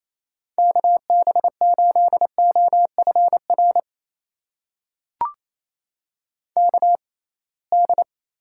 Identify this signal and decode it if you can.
Morse code, heard as a keyed tone, about 28 wpm, spaced as for 21 wpm, 700 Hz. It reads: KB8OFR KD
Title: USA Call Signs - 6 Characters - 6x Character Spacing - 28wpm